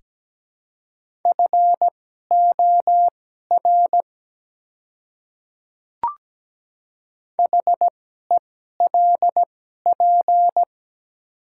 Morse code, heard as a keyed tone, about 17 wpm, spaced as for 12 wpm, 700 Hz. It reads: FOR HELP
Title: 2x Character Spacing - Top 200 Words - 17wpm